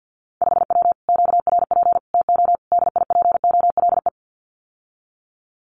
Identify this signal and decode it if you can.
Morse code, heard as a keyed tone, about 50 wpm, 700 Hz. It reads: 5W QRP TO DIPOLE
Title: Common QSO Phrases Encore 50wpm